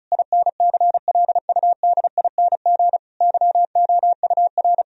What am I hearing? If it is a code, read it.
Morse code, heard as a keyed tone, about 36 wpm, 700 Hz. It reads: INCLUDING YOUR
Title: Sentences from Top 1000 Words - Review Part 4 - 35wpm